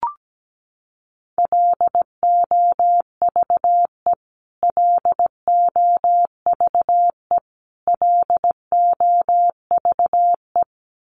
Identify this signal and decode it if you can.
Morse code, heard as a keyed tone, about 17 wpm, 700 Hz. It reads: LOVE LOVE LOVE
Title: Top 500 Words Repeated 17wpm